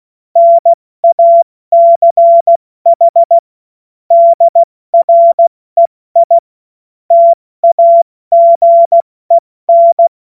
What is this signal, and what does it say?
Morse code, heard as a keyed tone, about 16 wpm, 680 Hz. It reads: NACH DREI TAGEN
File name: Witz-16-September-2025-70-F80